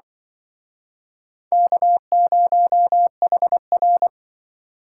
Morse code, heard as a keyed tone, about 25 wpm, 700 Hz. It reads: K0HR
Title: Speed Racing - USA 1x2 Call Signs 20wpm